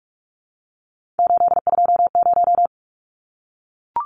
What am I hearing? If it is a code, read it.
Morse code, heard as a keyed tone, about 44 wpm, 700 Hz. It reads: 820
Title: Numbers - 3-Digit 45wpm